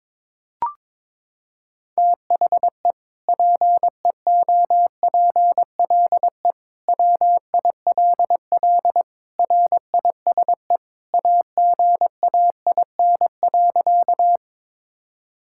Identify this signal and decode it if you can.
Morse code, heard as a keyed tone, about 22 wpm, 700 Hz. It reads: THE PEOPLE WILL RISE AGAIN.
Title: Sentences from Top 1000 Words - Review Part 6 - 22wpm